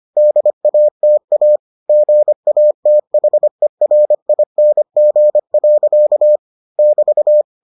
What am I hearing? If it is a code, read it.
Morse code, heard as a keyed tone, about 25 wpm, 600 Hz. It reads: DATA GATHERING. <BT>